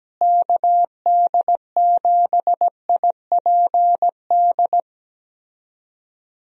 Morse code, heard as a keyed tone, about 17 wpm, 700 Hz. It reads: KD7IPD
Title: USA Call Signs - 6 Characters - Repeated 2x - 17wpm